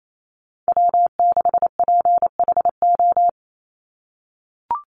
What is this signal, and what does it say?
Morse code, heard as a keyed tone, about 28 wpm, 700 Hz. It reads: W6PHO
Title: Call Signs - 28wpm